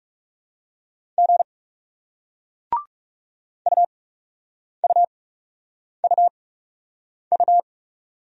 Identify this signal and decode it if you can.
Morse code, heard as a keyed tone, about 39 wpm, spaced as for 8 wpm, 700 Hz. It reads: G UUUU